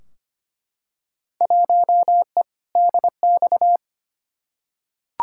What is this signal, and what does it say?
Morse code, heard as a keyed tone, about 25 wpm, 700 Hz. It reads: 1E DX
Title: Field Day Exchanges Encore - 25wpm